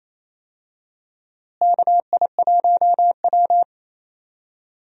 Morse code, heard as a keyed tone, about 28 wpm, 700 Hz. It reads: KI1W
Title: USA Call Signs - 4 Characters - Repeated 3x - 28wpm